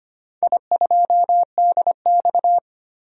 Morse code, heard as a keyed tone, about 25 wpm, 700 Hz. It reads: I2DX
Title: DXCC-100 Call Signs - 4 Characters - 25wpm